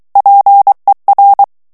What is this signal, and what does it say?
Morse code, heard as a keyed tone, about 23 wpm, 800 Hz. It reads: PER
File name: ITALIEN texte clair 2 vitesse 22 mots minute